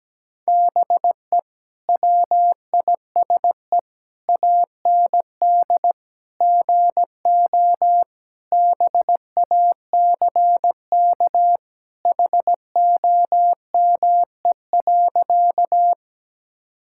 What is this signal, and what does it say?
Morse code, heard as a keyed tone, about 17 wpm, 700 Hz. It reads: BE WISE AND GO BACK HOME.
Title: Sentences from Top 1600 Words 17wpm